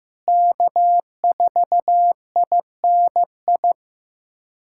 Morse code, heard as a keyed tone, about 15 wpm, 700 Hz. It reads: K4INI
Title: USA Callsign DE Callsign - 5 Characters - 15wpm